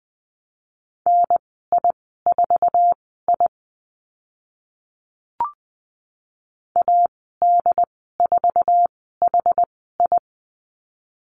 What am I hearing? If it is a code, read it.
Morse code, heard as a keyed tone, about 20 wpm, spaced as for 14 wpm, 700 Hz. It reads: NI4I AD4HI
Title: 082 Lesson 14g - Number 2 - Call Signs 20wpm